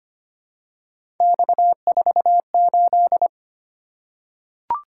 Morse code, heard as a keyed tone, about 25 wpm, 700 Hz. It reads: X48
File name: Three Letter-Number Combinations - 25wpm